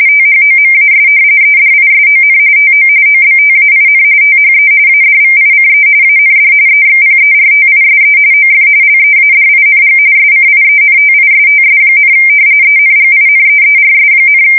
RTTY, about 45.45 baud. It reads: THOMAS EDISON TO OBTAIN THE ADVICE AND EXPERTISE OF LEADING SCIENTISTS, ENGINEERS,
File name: fsk170 / usnavy4